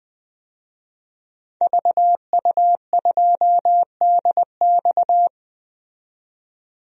Morse code, heard as a keyed tone, about 21 wpm, 700 Hz. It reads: VU2DX
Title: DXCC-100 Call Signs - 5 Characters - 20wpm